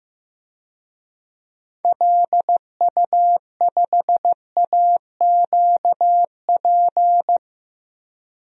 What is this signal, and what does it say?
Morse code, heard as a keyed tone, about 15 wpm, 700 Hz. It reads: LU5AQP
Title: DXCC-100 Call Signs - 6 Characters - 15wpm